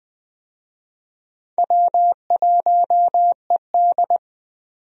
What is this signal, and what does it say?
Morse code, heard as a keyed tone, about 20 wpm, 700 Hz. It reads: W1ED
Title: USA Call Signs - 4 Characters - 20wpm